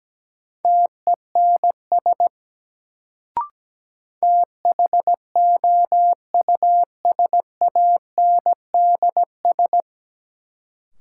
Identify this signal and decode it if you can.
Morse code, heard as a keyed tone, about 17 wpm, 700 Hz. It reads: TENS THOUSANDS
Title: Sets of 4 Words - New Word Drill - 17wpm